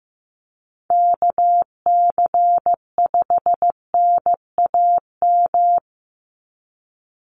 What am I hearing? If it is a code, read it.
Morse code, heard as a keyed tone, about 15 wpm, 700 Hz. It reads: KC5NAM
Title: USA Call Signs 15wpm